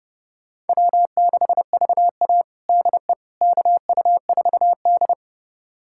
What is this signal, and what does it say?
Morse code, heard as a keyed tone, about 30 wpm, 700 Hz. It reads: W6VA DE KU4D
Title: USA Callsign DE Callsign - 4 Characters - 30wpm